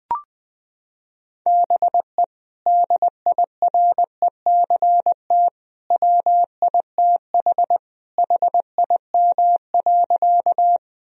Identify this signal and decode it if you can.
Morse code, heard as a keyed tone, about 20 wpm, 700 Hz. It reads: BE DIRECT WITH HIM.
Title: Sentences from Top 1000 Words - Review Part 2 - 20wpm